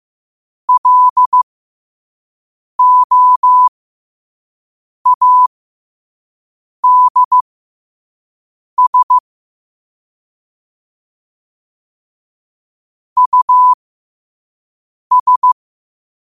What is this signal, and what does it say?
Morse code, heard as a keyed tone, about 15 wpm, spaced as for 5 wpm, 1000 Hz. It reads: LOADS US
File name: qotdfiles / 5wpm